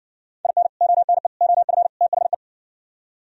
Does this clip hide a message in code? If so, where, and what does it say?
Yes: Morse code from 0.4 to 2.3 s, heard as a keyed tone, about 60 wpm, 700 Hz. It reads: IN ONE OF THE